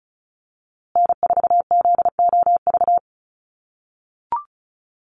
Morse code, heard as a keyed tone, about 34 wpm, 700 Hz. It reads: N4ZOV
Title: USA Call Signs Repeated 3x 35wpm